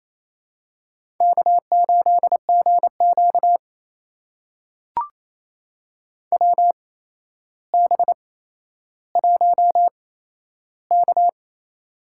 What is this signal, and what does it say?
Morse code, heard as a keyed tone, about 28 wpm, spaced as for 12 wpm, 700 Hz. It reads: K8GQ WB1K
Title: USA Call Signs - 4 Characters - 8x Character Spacing - 28wpm